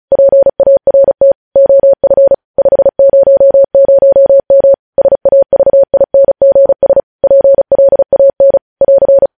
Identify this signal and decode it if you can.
Morse code, heard as a keyed tone, about 36 wpm, 550 Hz. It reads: PART OF 500M SAVINGS PLAN <AR>
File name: podcast / News headlines-1x-35x35